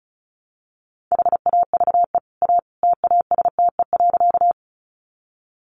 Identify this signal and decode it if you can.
Morse code, heard as a keyed tone, about 34 wpm, 700 Hz. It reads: HAVE A TASTE.
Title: Sentences from Top 1100 Words 35wpm